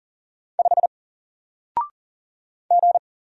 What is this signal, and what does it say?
Morse code, heard as a keyed tone, about 40 wpm, 700 Hz. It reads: 5 G
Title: Single Letter-Number Alt 40wpm